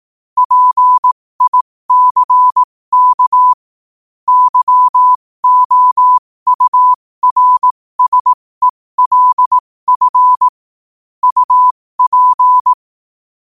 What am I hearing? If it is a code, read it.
Morse code, heard as a keyed tone, about 18 wpm, spaced as for 15 wpm, 1000 Hz. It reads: PICK YOURSELF UP